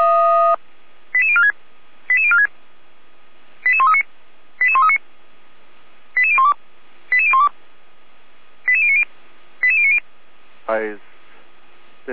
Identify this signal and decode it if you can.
Five-tone selective call (ZVEI), heard as quick runs of five tones, tones of about 70 ms, with speech. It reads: 88046 88046 88139 88139 88012 88012 88090 88090